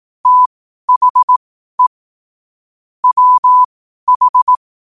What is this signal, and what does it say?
Morse code, heard as a keyed tone, about 18 wpm, spaced as for 12 wpm, 1000 Hz. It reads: THE WH
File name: qotdfiles / 12wpm